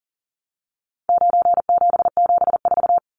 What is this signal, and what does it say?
Morse code, heard as a keyed tone, about 38 wpm, 700 Hz. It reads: 9774
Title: Numbers - 4-Digit 40wpm